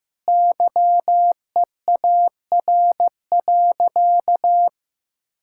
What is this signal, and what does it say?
Morse code, heard as a keyed tone, about 16 wpm, 700 Hz. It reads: YEAR.
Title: Sentences from Top 1000 Words - Review Part 5 - 15wpm